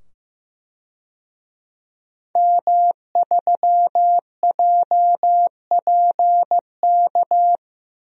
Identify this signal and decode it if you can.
Morse code, heard as a keyed tone, about 15 wpm, 700 Hz. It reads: M3JPK
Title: DXCC-100 Call Signs - 5 Characters - 15wpm